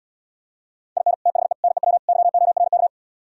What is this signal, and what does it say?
Morse code, heard as a keyed tone, about 77 wpm, 700 Hz. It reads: IN THE NEW YORK